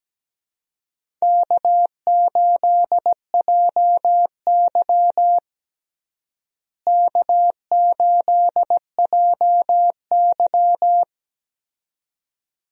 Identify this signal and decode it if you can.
Morse code, heard as a keyed tone, about 17 wpm, 700 Hz. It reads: K8JY K8JY
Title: USA Call Signs - 4 Characters - Repeated 2x - 17wpm